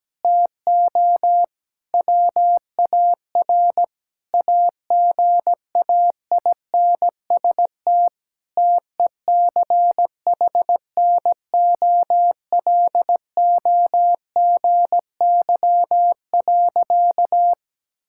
Morse code, heard as a keyed tone, about 17 wpm, 700 Hz. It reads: TO WAR AGAINST TECHNOLOGY.
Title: Sentences from Top 1000 Words - Review Part 5 - 17wpm